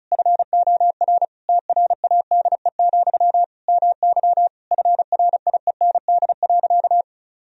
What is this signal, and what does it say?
Morse code, heard as a keyed tone, about 36 wpm, 700 Hz. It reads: FOR TRADE, MY FRIEND.
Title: Sentences from Top 1000 Words - Review Part 4 - 35wpm